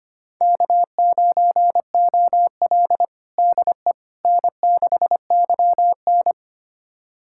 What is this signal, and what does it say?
Morse code, heard as a keyed tone, about 25 wpm, 700 Hz. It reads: K9OL DE N6YN
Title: USA Callsign DE Callsign - 4 Characters - 25wpm